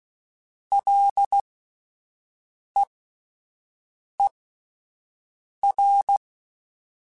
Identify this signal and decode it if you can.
Morse code, heard as a keyed tone, about 16 wpm, spaced as for 5 wpm, 800 Hz. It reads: LEER